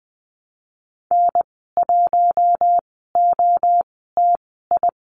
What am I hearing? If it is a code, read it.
Morse code, heard as a keyed tone, about 20 wpm, spaced as for 15 wpm, 700 Hz. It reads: N1OTI
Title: 062 Lesson 11g - Letter h - Call Signs 20wpm